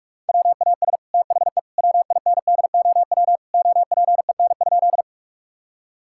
Morse code, heard as a keyed tone, about 43 wpm, 700 Hz. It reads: WAS THE WINDOW OPEN?
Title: Sentences from Top 1600 Words 45wpm